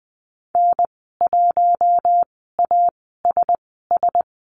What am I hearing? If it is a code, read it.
Morse code, heard as a keyed tone, about 20 wpm, spaced as for 14 wpm, 700 Hz. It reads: N1ASS
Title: 042 Lesson 8g - Number 1 - Call Signs 20wpm